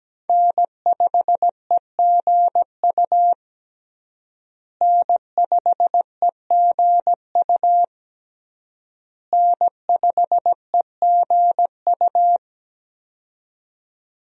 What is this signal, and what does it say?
Morse code, heard as a keyed tone, about 17 wpm, 700 Hz. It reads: N5EGU N5EGU N5EGU
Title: USA Call Signs - 5 Characters - Repeated 3x - 17wpm